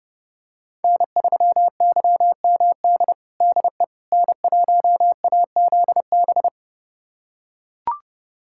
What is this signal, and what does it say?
Morse code, heard as a keyed tone, about 30 wpm, 700 Hz. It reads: N3YMD DE N1AZB
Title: USA Callsign DE Callsign - 5 Characters - 30wpm